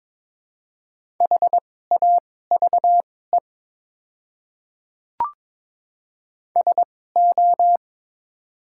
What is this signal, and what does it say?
Morse code, heard as a keyed tone, about 22 wpm, spaced as for 16 wpm, 700 Hz. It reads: HAVE SO
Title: 2x Character Spacing - Top 100 Words - 22wpm